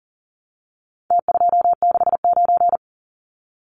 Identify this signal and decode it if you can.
Morse code, heard as a keyed tone, about 40 wpm, 700 Hz. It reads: T269
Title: Numbers - Contest Style - 4-Digit 40wpm